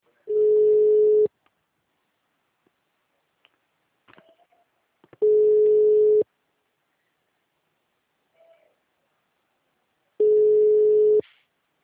Ringback tone (Europe, 425 Hz).